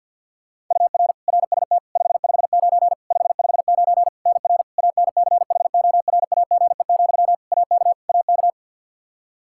Morse code, heard as a keyed tone, about 50 wpm, 700 Hz. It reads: UR RST 559 559 NR ANCHORAGE, AK AK
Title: Practice QSO - Exchange 1 - Beginning - 50wpm